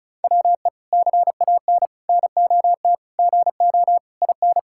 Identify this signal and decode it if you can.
Morse code, heard as a keyed tone, about 34 wpm, 700 Hz. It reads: WE CAN NOT GO IN
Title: Sentences from Top 1000 Words - Review Part 6 - 35wpm